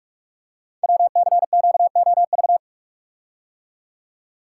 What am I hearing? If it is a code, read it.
Morse code, heard as a keyed tone, about 44 wpm, 700 Hz. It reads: WCQOV
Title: Code Groups - 5 Letters - 45wpm